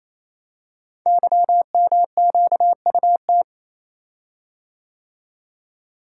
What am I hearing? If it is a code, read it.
Morse code, heard as a keyed tone, about 27 wpm, 700 Hz. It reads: YMQUT
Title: Code Groups - 5 Letters - 28wpm